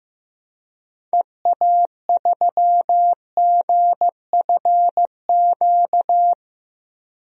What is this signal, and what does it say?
Morse code, heard as a keyed tone, about 15 wpm, 700 Hz. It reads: EA3GFQ
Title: DXCC-100 Call Signs - 6 Characters - 15wpm